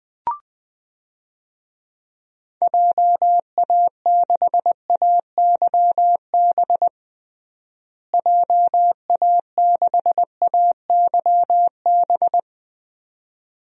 Morse code, heard as a keyed tone, about 20 wpm, 700 Hz. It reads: JA6AYB JA6AYB
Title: DXCC-100 Call Signs - 6 Characters - Repeated 2x - 20wpm